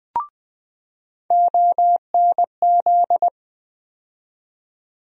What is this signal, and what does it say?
Morse code, heard as a keyed tone, about 20 wpm, 700 Hz. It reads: ONZ
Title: USA Call Sign Postfixes - 3 Letters 20wpm